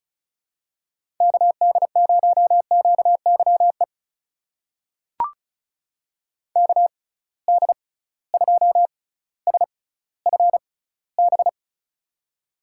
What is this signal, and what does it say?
Morse code, heard as a keyed tone, about 36 wpm, spaced as for 18 wpm, 700 Hz. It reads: KD0QYE KD2SFB
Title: USA Call Signs - 6 Characters - 6x Character Spacing - 35wpm